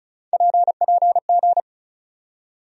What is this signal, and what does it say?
Morse code, heard as a keyed tone, about 34 wpm, 700 Hz. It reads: PPG
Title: USA Call Sign Postfixes - 3 Letters 35wpm